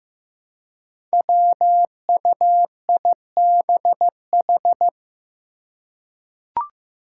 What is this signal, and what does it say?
Morse code, heard as a keyed tone, about 15 wpm, 700 Hz. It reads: WUIBH
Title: Five Challenging Letter-Number Combinations 15wpm